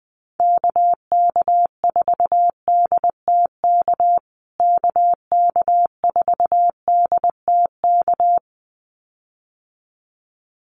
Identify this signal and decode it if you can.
Morse code, heard as a keyed tone, about 20 wpm, 700 Hz. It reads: KK4DTK KK4DTK
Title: USA Call Signs Repeated 2x 20wpm